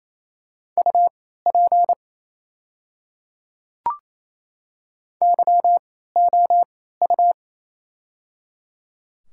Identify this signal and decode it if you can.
Morse code, heard as a keyed tone, about 28 wpm, spaced as for 16 wpm, 700 Hz. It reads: UP YOU
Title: 3x Character Spacing - Top 100 Words - 28wpm